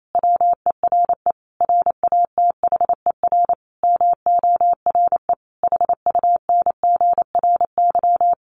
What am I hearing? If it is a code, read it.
Morse code, heard as a keyed tone, about 28 wpm, 700 Hz. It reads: WERE RATHER MORE HUNGRY
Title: Sentences from Top 1400 Words 28wpm